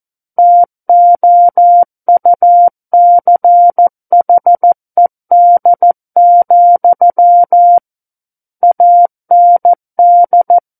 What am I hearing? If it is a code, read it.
Morse code, heard as a keyed tone, about 14 wpm, 700 Hz. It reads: TOUCHED, AND
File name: Book One / 05-Chapter 5 - 14 WPM